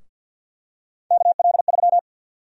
Morse code, heard as a keyed tone, about 53 wpm, 700 Hz. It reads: KL3
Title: USA Call Sign Prefixes - 50wpm